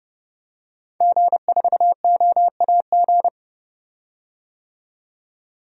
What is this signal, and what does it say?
Morse code, heard as a keyed tone, about 32 wpm, 700 Hz. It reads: G4OAG
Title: DXCC-100 Call Signs - 5 Characters - 30wpm